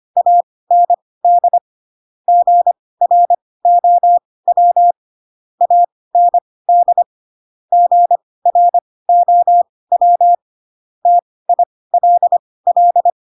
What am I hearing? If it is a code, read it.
Morse code, heard as a keyed tone, about 25 wpm, spaced as for 18 wpm, 700 Hz. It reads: AND GROW AND GROW TILL
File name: Pinocchio ch3 12 25 ik1hge